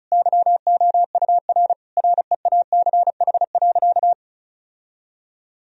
Morse code, heard as a keyed tone, about 36 wpm, 700 Hz. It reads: YOUR REACH.
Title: Sentences from Top 1000 Words - Review Part 2 - 35wpm